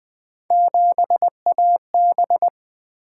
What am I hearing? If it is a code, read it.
Morse code, heard as a keyed tone, about 20 wpm, 700 Hz. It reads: 7AB
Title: Three Letter-Number Combinations - 20wpm